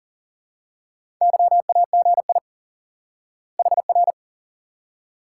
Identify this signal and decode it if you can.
Morse code, heard as a keyed tone, about 40 wpm, 700 Hz. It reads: YAGI HR
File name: IQE Course 3 - Lesson 89 - YAGI - Two Elements - 40wpm